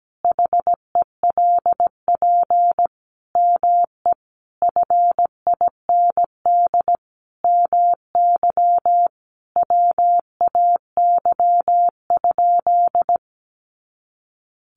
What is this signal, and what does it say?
Morse code, heard as a keyed tone, about 17 wpm, 700 Hz. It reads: HELP ME FIND MY WAY?
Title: Sentences from Top 200 Words 17wpm